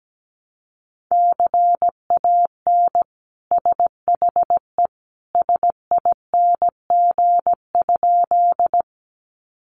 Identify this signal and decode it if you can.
Morse code, heard as a keyed tone, about 17 wpm, 700 Hz. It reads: CAN SHE SING?
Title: Sentences from Top 400 Words 17wpm